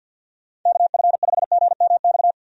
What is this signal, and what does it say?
Morse code, heard as a keyed tone, about 46 wpm, 700 Hz. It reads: KF5GMX
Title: USA Call Signs - 6 Characters - 50wpm